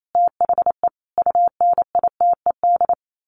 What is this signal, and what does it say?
Morse code, heard as a keyed tone, about 28 wpm, 700 Hz. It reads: THE UNITED
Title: Sets of 5 Words Encore2 28wpm